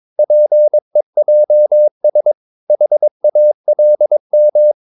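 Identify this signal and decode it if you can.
Morse code, heard as a keyed tone, about 22 wpm, 600 Hz. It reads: PEJS HALM
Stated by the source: Morse practice of short Danish words at 22 wpm, from CW med Gnister.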